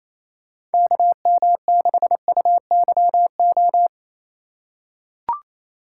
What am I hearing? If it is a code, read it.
Morse code, heard as a keyed tone, about 28 wpm, 690 Hz. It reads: KM6UYO
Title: USA Call Signs - 6 Characters - 4x Character Spacing - 28wpm